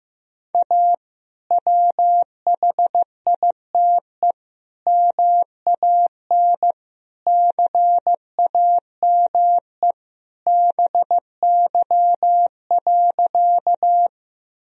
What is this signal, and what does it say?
Morse code, heard as a keyed tone, about 15 wpm, 700 Hz. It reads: A WHITE MAN CAME BY.
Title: Sentences from Top 1000 Words - Review Part 1 - 15wpm